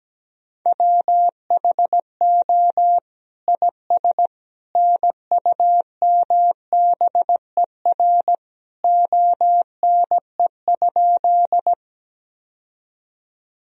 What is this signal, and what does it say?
Morse code, heard as a keyed tone, about 17 wpm, 700 Hz. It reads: WHO IS NUMBER ONE?
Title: Sentences from Top 1000 Words - Review Part 1 - 17wpm